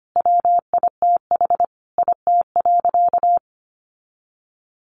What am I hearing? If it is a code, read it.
Morse code, heard as a keyed tone, about 25 wpm, 700 Hz. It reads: WITH IT.
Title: Sentences from Top 1500 Words 25wpm